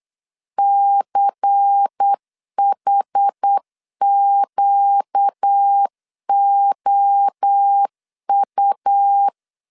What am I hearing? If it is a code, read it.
Morse code, heard as a keyed tone, about 8 wpm, 800 Hz. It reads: CHQOU
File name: 8wpm